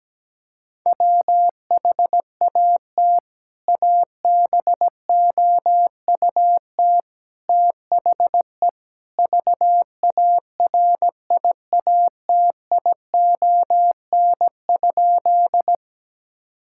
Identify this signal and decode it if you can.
Morse code, heard as a keyed tone, about 17 wpm, 700 Hz. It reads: WHAT ABOUT THE VARIATION?
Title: Sentences from Top 1600 Words 17wpm